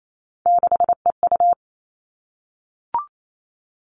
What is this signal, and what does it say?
Morse code, heard as a keyed tone, about 28 wpm, 700 Hz. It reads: 6EU
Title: All Cut Numbers - 3-Digit 28wpm